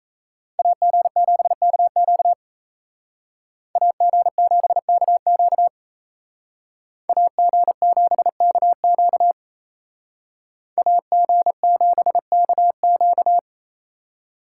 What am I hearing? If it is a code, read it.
Morse code, heard as a keyed tone, about 36 wpm, 700 Hz. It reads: AG7KQ AG7KQ AG7KQ AG7KQ